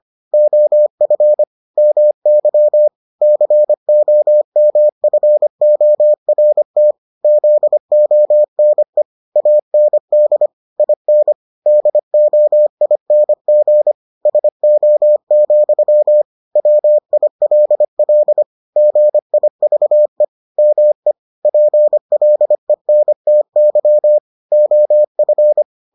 Morse code, heard as a keyed tone, about 25 wpm, 600 Hz. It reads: OF MY COMFORT ZONE AND IN DOING SO, WILL GIVE ME PLENTY OF